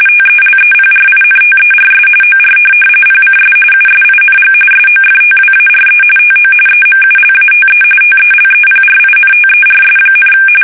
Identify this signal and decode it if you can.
RTTY, about 45.45 baud. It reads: TELEPRINTERS OR PCS USING SOFTWARE TO EMULATE TELEPRINTERS,
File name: fsk850 / teletype-RTTY